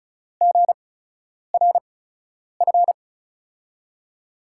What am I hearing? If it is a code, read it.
Morse code, heard as a keyed tone, about 36 wpm, spaced as for 10 wpm, 700 Hz. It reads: GRF
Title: USA Call Signs - 8x Character Spacing - 35wpm